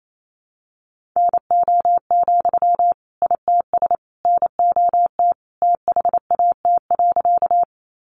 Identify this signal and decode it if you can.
Morse code, heard as a keyed tone, about 28 wpm, 700 Hz. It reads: NO, ITS NOT THAT.
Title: Sentences from Top 600 Words 28wpm